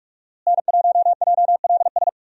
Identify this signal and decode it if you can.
Morse code, heard as a keyed tone, about 42 wpm, 700 Hz. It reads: N1JLS
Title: USA Call Signs - 5 Characters - Repeated 2x - 45wpm